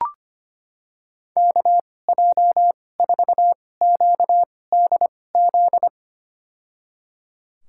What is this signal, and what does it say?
Morse code, heard as a keyed tone, about 25 wpm, spaced as for 18 wpm, 700 Hz. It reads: KJ4QDZ
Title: USA Call Signs - 6 Characters - 2x Character Spacing - 25wpm